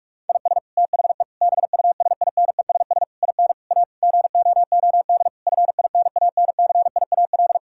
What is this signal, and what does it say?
Morse code, heard as a keyed tone, about 41 wpm, 700 Hz. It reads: IS THE BUSINESS IN A GOOD FINANCIAL
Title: Sentences from Top 1000 Words - Review Part 5 - 45wpm